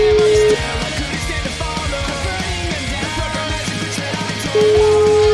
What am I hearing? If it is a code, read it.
Ringback tone (Europe, 425 Hz), with noise.